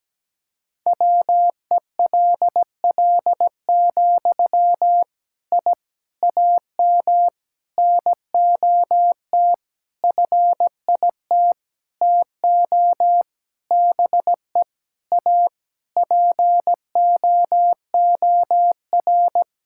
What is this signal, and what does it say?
Morse code, heard as a keyed tone, about 17 wpm, 700 Hz. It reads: WELL, I AM NOT FIT TO BE A POOR
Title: Sentences from Top 1000 Words - Review Part 5 - 17wpm